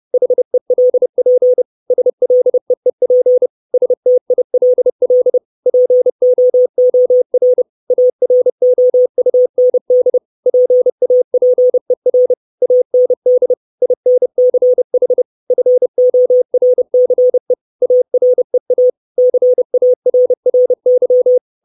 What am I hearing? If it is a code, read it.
Morse code, heard as a keyed tone, about 30 wpm, 500 Hz. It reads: HELP SLEEP STILL POOR AROUND PAPER AND INCH FORCE AREA CARRY
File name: 632 Words 30WPM